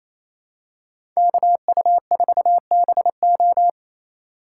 Morse code, heard as a keyed tone, about 28 wpm, 700 Hz. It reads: KU4BO
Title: USA Call Signs - 5 Characters - 2x Character Spacing - 28wpm